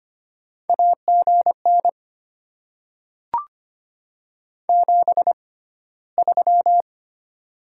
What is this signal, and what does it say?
Morse code, heard as a keyed tone, about 25 wpm, spaced as for 15 wpm, 700 Hz. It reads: AGN 73
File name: Top 20 QSO Elements - 6x Character Spacing - 25wpm